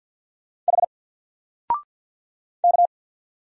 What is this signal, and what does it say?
Morse code, heard as a keyed tone, about 50 wpm, 700 Hz. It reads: H K
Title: Single Character (Letter:Number:Punctuation:Prosign) 50wpm